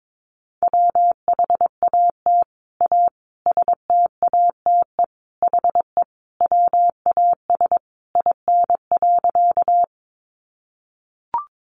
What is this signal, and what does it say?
Morse code, heard as a keyed tone, about 22 wpm, 700 Hz. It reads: WHAT A STATE HE WAS IN.
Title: Sentences from Top 400 Words 22wpm